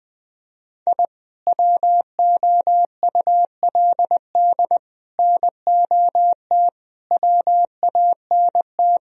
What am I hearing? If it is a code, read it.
Morse code, heard as a keyed tone, about 20 wpm, 700 Hz. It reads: I WOULD NOT WANT
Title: Speed-Racing - Sets of 5 Words 20wpm